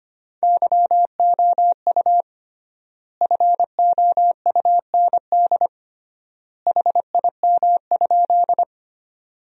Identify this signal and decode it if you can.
Morse code, heard as a keyed tone, about 25 wpm, 700 Hz. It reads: YOU FOUND HIM?
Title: 3x Word-Spacing - Sentences from Top 300 Words - 25wpm